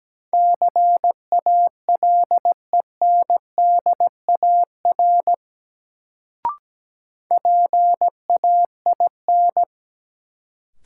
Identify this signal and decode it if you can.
Morse code, heard as a keyed tone, about 17 wpm, 700 Hz. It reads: CALENDAR PAIN
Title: Sentences from Top 1600 Words - New Word Drill Alt - 17wpm